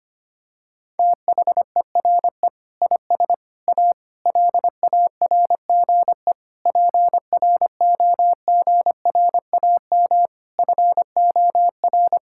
Morse code, heard as a keyed tone, about 25 wpm, 700 Hz. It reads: THERE IS A LARGE PROGRAM FOR
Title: Sentences from Top 1000 Words - Review Part 5 - 25wpm